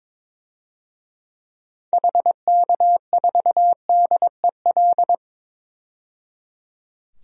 Morse code, heard as a keyed tone, about 22 wpm, 700 Hz. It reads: HK4DEL
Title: DXCC-100 Call Signs - 6 Characters - 22wpm